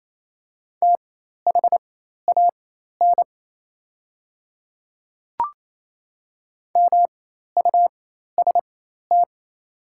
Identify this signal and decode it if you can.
Morse code, heard as a keyed tone, about 27 wpm, spaced as for 13 wpm, 690 Hz. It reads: THAN MUST
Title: 4x Character Spacing - Top 200 Words - 28wpm